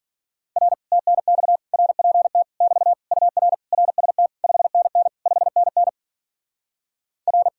Morse code, heard as a keyed tone, about 53 wpm, 700 Hz. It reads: R TNX RPT <BT> UR RST 5NN 5NN R